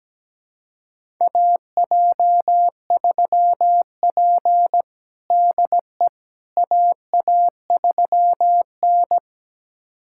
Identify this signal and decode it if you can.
Morse code, heard as a keyed tone, about 17 wpm, 700 Hz. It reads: AJ3P DE AA3N
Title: USA Callsign DE Callsign - 4 Characters - 17wpm